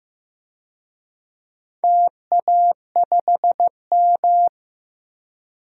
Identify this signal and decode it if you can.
Morse code, heard as a keyed tone, about 15 wpm, 700 Hz. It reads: TA5M
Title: DXCC-100 Call Signs - 4 Characters - Repeated 2x - 15wpm